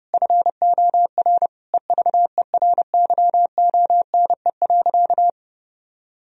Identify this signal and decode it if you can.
Morse code, heard as a keyed tone, about 30 wpm, 700 Hz. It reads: FOR EVERYONE.
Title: Sentences from Top 1000 Words - Review Part 3 - 30wpm